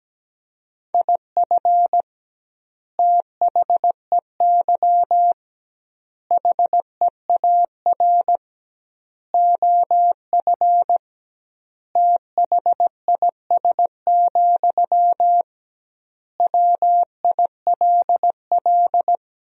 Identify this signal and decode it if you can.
Morse code, heard as a keyed tone, about 17 wpm, 700 Hz. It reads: IF THEY HEAR OF THIS, WILL
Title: 2x Word-Spacing - Sentences from Top 400 Words - 17wpm